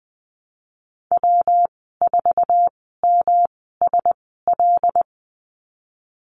Morse code, heard as a keyed tone, about 20 wpm, spaced as for 14 wpm, 700 Hz. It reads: W4MSL
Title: 128 Lesson 21g - Number 6 - Call Signs 20wpm